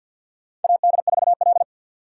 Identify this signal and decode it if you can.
Morse code, heard as a keyed tone, about 52 wpm, 700 Hz. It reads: AD4L
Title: USA Call Signs - 2x Character Spacing - 50wpm